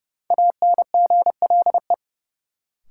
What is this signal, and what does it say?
Morse code, heard as a keyed tone, about 32 wpm, 700 Hz. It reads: ANGLE
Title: Sentences from Top 1500 Words - New Word Drill - 30wpm